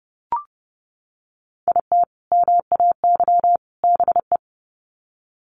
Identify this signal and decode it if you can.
Morse code, heard as a keyed tone, about 30 wpm, 700 Hz. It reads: IT MAY BE
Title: Sets of 3 Words 30wpm